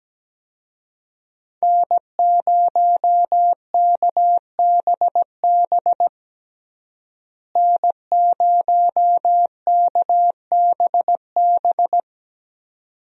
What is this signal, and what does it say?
Morse code, heard as a keyed tone, about 17 wpm, 700 Hz. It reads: N0KBB N0KBB
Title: USA Call Signs - 5 Characters - Repeated 2x - 17wpm